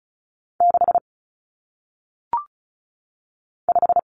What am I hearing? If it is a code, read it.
Morse code, heard as a keyed tone, about 36 wpm, 700 Hz. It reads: 6 5
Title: Numbers - 1-Digit 35wpm